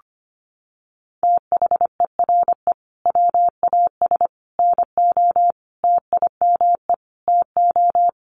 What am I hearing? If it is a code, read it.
Morse code, heard as a keyed tone, about 25 wpm, 700 Hz. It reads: THERE WAS NO TIME TO
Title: Sentences from Top 1100 Words 25wpm